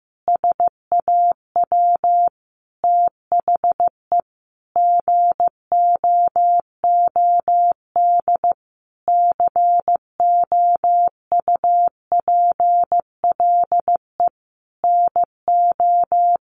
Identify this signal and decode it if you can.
Morse code, heard as a keyed tone, about 14 wpm, 700 Hz. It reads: SAW THE GOOD COUPLE NO
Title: Sentences from Top 1100 Words 15wpm